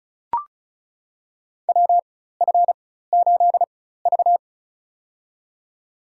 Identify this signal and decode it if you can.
Morse code, heard as a keyed tone, about 34 wpm, spaced as for 16 wpm, 700 Hz. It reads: WF8V
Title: USA Call Signs - 4 Characters - 4x Character Spacing - 35wpm